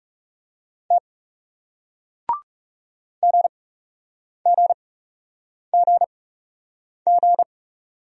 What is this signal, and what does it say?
Morse code, heard as a keyed tone, about 39 wpm, spaced as for 8 wpm, 700 Hz. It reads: T GGGG